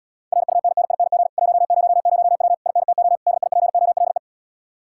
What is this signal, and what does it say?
Morse code, heard as a keyed tone, about 80 wpm, 700 Hz. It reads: RUNNING 100W INTO DIPOLE